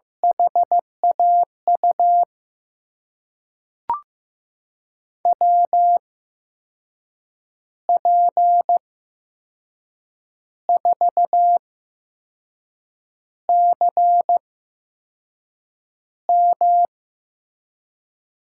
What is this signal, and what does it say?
Morse code, heard as a keyed tone, about 16 wpm, spaced as for 5 wpm, 700 Hz. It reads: HAU WP4CM
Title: USA Call Signs - 8x Character Spacing - 15wpm